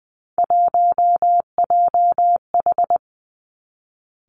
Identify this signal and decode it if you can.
Morse code, heard as a keyed tone, about 20 wpm, 700 Hz. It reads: 1JH
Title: Three Challenging Letter-Number Combinations 20wpm